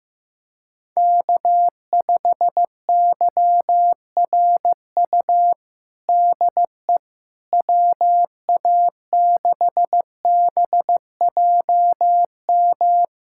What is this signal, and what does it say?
Morse code, heard as a keyed tone, about 15 wpm, 700 Hz. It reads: K5YRU DE WA6BJM
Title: USA Callsign DE Callsign 15wpm